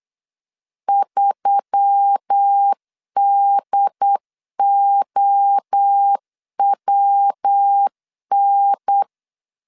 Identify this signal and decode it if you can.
Morse code, heard as a keyed tone, about 8 wpm, 800 Hz. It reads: 3DOWN
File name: LEZIONI / 6 PKX0 / 6 P 8wpm 3025